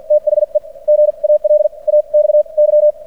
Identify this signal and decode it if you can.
Morse code, heard as a keyed tone, about 56 wpm, spaced as for 40 wpm, 600 Hz. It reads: THE GALAXY
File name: archived / CRSnet